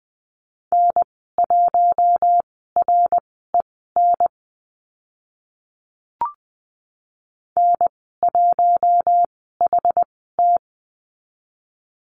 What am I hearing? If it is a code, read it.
Morse code, heard as a keyed tone, about 20 wpm, spaced as for 15 wpm, 700 Hz. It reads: N1REN N1HT
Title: 062 Lesson 11g - Letter h - Call Signs 20wpm